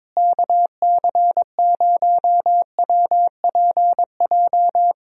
Morse code, heard as a keyed tone, about 22 wpm, 700 Hz. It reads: KC0WPJ
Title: USA Call Signs - 6 Characters - Repeated 2x - 22wpm